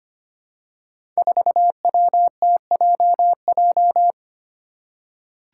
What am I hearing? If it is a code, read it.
Morse code, heard as a keyed tone, about 25 wpm, 700 Hz. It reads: 4WTJJ
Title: Five Challenging Letter-Number Combinations 25wpm